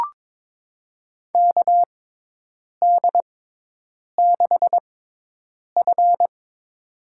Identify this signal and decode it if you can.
Morse code, heard as a keyed tone, about 22 wpm, spaced as for 8 wpm, 700 Hz. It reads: KD6F
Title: USA Call Signs - 4 Characters - 6x Character Spacing - 22wpm